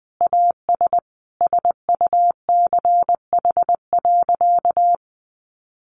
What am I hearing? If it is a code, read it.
Morse code, heard as a keyed tone, about 20 wpm, 700 Hz. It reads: AS SUCH.
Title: Sentences from Top 1300 Words 20wpm